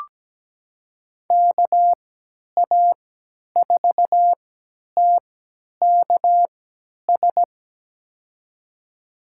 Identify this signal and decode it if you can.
Morse code, heard as a keyed tone, about 17 wpm, spaced as for 10 wpm, 700 Hz. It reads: KA4TKS
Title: USA Call Signs - 3x Character Spacing - 17wpm